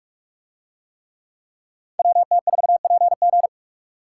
Morse code, heard as a keyed tone, about 43 wpm, 700 Hz. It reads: WT4PG